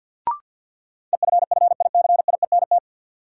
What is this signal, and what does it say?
Morse code, heard as a keyed tone, about 52 wpm, 700 Hz. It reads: EFFICIENT
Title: Sentences from Top 1300 Words - New Word Drill - 50wpm